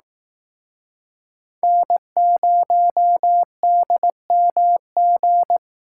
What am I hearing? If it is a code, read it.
Morse code, heard as a keyed tone, about 18 wpm, 700 Hz. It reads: N0DMG
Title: Speed Racing - USA Call Signs 15wpm